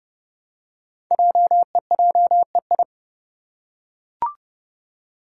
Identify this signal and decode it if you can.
Morse code, heard as a keyed tone, about 30 wpm, 700 Hz. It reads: JEJEI
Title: Code Groups - 5 Letter:Numbers - 30wpm